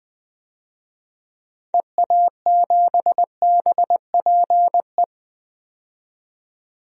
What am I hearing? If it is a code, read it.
Morse code, heard as a keyed tone, about 20 wpm, 700 Hz. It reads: EA7BPE
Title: DXCC-100 Call Signs - 6 Characters - 20wpm